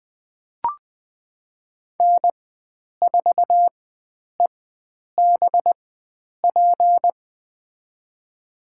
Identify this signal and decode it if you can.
Morse code, heard as a keyed tone, about 20 wpm, spaced as for 9 wpm, 700 Hz. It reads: N4EBP
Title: USA Call Signs - 5 Characters - 4x Character Spacing - 20wpm